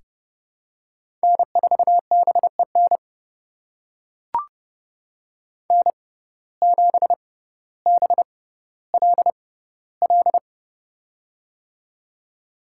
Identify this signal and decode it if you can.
Morse code, heard as a keyed tone, about 32 wpm, spaced as for 16 wpm, 700 Hz. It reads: N4BEN N7BLL
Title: USA Call Signs - 5 Characters - 6x Character Spacing - 30wpm